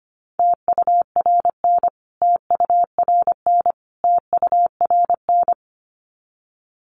Morse code, heard as a keyed tone, about 25 wpm, 700 Hz. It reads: TURN TURN TURN
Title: Top 200 Words Repeated 25wpm